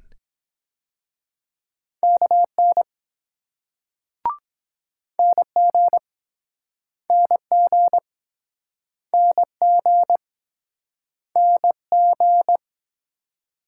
Morse code, heard as a keyed tone, about 23 wpm, 700 Hz. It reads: KN NG NG NG NG